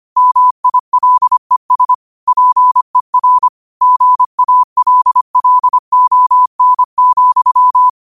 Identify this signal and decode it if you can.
Morse code, heard as a keyed tone, about 25 wpm, 1000 Hz. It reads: MILES PER GALLON,
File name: qotdfiles / 25wpm